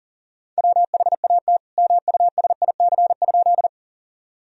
Morse code, heard as a keyed tone, about 40 wpm, 700 Hz. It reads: WHAT MUSIC?
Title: Sentences from Top 1000 Words - Review Part 1 - 40wpm